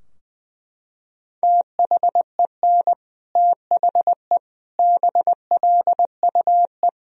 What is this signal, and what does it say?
Morse code, heard as a keyed tone, about 20 wpm, 700 Hz. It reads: THEN THE BLUE
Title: Sentences from Top 1000 Words - Review Part 3 - 20wpm